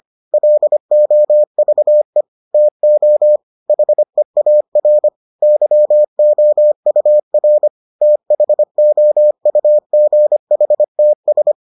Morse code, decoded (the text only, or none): LOVE TO HEAR YOUR THOUGHTS